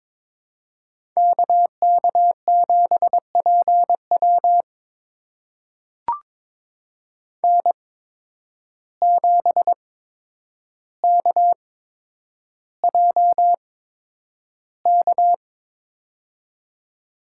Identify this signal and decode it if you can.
Morse code, heard as a keyed tone, about 22 wpm, spaced as for 9 wpm, 700 Hz. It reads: KK7PW N7KJK
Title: USA Call Signs - 5 Characters - 8x Character Spacing - 22wpm